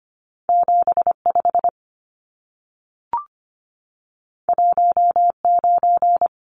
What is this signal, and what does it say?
Morse code, heard as a keyed tone, about 25 wpm, 700 Hz. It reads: 75 19
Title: Numbers - 2-Digit 25wpm